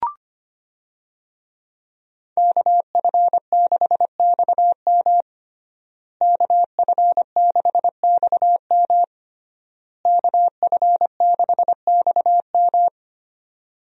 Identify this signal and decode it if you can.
Morse code, heard as a keyed tone, about 25 wpm, 700 Hz. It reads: KF6XM KF6XM KF6XM
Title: USA Call Signs - 5 Characters - Repeated 3x - 25wpm